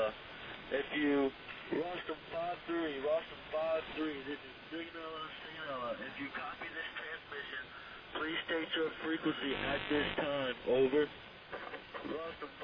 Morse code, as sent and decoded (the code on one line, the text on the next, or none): none